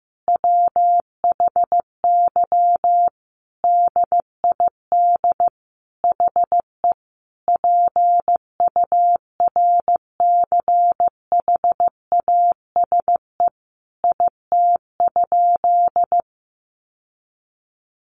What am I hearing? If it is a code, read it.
Morse code, heard as a keyed tone, about 15 wpm, 700 Hz. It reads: WHY DID HE PURCHASE IT?
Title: Sentences from Top 800 Words 15wpm